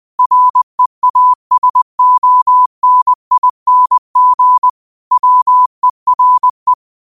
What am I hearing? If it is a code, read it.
Morse code, heard as a keyed tone, about 20 wpm, 1000 Hz. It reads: REASONING WERE
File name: qotdfiles / 20wpm